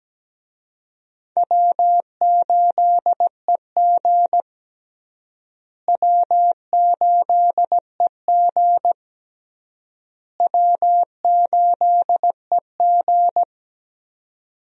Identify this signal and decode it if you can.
Morse code, heard as a keyed tone, about 17 wpm, 700 Hz. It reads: W8EG W8EG W8EG
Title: USA Call Signs - 4 Characters - Repeated 3x - 17wpm